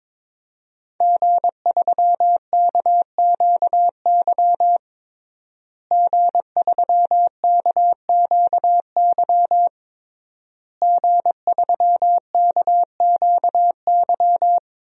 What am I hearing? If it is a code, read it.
Morse code, heard as a keyed tone, about 22 wpm, 700 Hz. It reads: G3KQY G3KQY G3KQY
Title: DXCC-100 Call Signs - 5 Characters - Repeated 3x - 22wpm